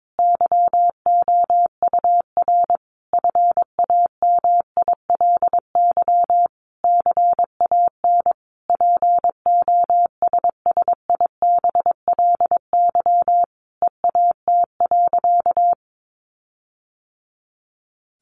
Morse code, heard as a keyed tone, about 22 wpm, 700 Hz. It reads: YOUR FAMILY CAN POSSIBLY EAT.
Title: Sentences from Top 1300 Words 22wpm